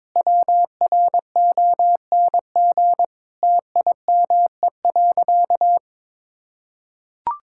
Morse code, heard as a keyed tone, about 22 wpm, 700 Hz. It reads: WRONG TIME.
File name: practice / 22wpm / Sentences with 8-Word Phrases - 22wpm